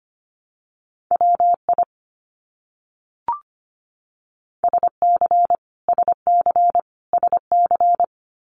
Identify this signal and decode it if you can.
Morse code, heard as a keyed tone, about 25 wpm, 700 Hz. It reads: WI SC SC SC
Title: US States Repeated 3x 25wpm